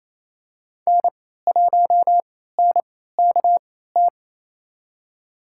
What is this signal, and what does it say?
Morse code, heard as a keyed tone, about 28 wpm, spaced as for 16 wpm, 700 Hz. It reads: N1NKT
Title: USA Call Signs - 3x Character Spacing - 28wpm